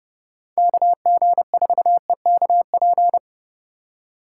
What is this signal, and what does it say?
Morse code, heard as a keyed tone, about 30 wpm, 700 Hz. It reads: KG4EKP